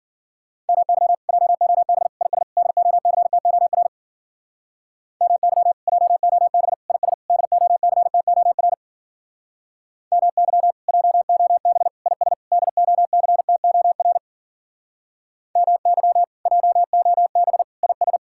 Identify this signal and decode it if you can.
Morse code, heard as a keyed tone, about 51 wpm, 700 Hz. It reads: MY JOB IS DOCTOR MY JOB IS DOCTOR MY JOB IS DOCTOR MY JOB IS